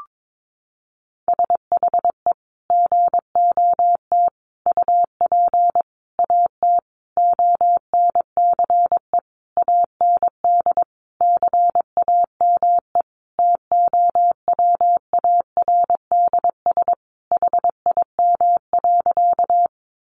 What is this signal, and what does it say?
Morse code, heard as a keyed tone, about 22 wpm, 700 Hz. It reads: SHE GOT UP AT ONCE AND CAME TOWARDS HIM.
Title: Sentences from Top 1100 Words 22wpm